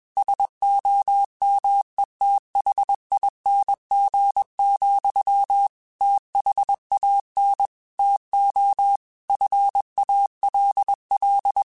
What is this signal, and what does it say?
Morse code, heard as a keyed tone, about 21 wpm, 800 Hz. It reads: SOMETHING, THAN TO FALL
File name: Quotes / Quotes-mp3s / StandFor @20